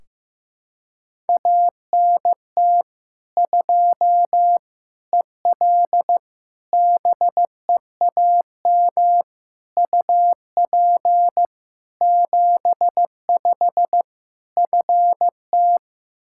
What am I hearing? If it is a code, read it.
Morse code, heard as a keyed tone, about 16 wpm, 700 Hz. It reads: ANT 2 EL BEAM UP 75 FT
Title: K1USN SST QSO Practice 15wpm